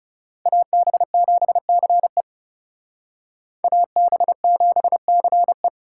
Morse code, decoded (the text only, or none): AB7CE AB7CE